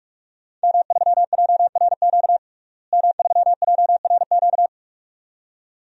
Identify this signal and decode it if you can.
Morse code, heard as a keyed tone, about 46 wpm, 700 Hz. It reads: M3JRQ M3JRQ
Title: DXCC-100 Call Signs - 5 Characters - Repeated 2x - 45wpm